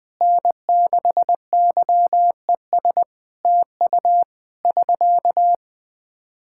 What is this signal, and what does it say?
Morse code, heard as a keyed tone, about 20 wpm, 700 Hz. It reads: N6YES TU <SK>
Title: Practice QSO - Exchange 4 - Ending - 20wpm